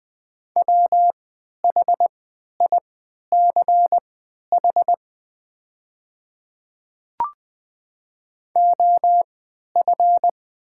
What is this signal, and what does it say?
Morse code, heard as a keyed tone, about 21 wpm, spaced as for 11 wpm, 700 Hz. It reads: WHICH OF
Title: 3x Character Spacing - Top 100 Words - 20wpm